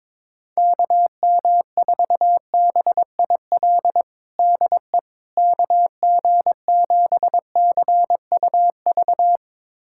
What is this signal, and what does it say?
Morse code, heard as a keyed tone, about 22 wpm, 700 Hz. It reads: KM4BIL DE KG7CUV
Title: USA Callsign DE Callsign - 6 Characters - 22wpm